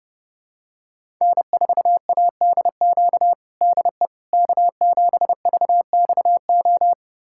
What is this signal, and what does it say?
Morse code, heard as a keyed tone, about 30 wpm, 700 Hz. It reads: N4ADQ DE K7VXO
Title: USA Callsign DE Callsign - 5 Characters - 30wpm